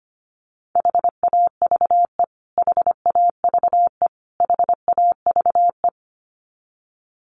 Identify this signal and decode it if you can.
Morse code, heard as a keyed tone, about 25 wpm, 700 Hz. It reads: HAVE HAVE HAVE
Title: Top 500 Words Repeated 25wpm